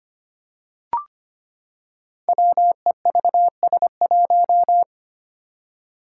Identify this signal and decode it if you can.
Morse code, heard as a keyed tone, about 25 wpm, 700 Hz. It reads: WEVS1
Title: Five Challenging Letter-Number Combinations 25wpm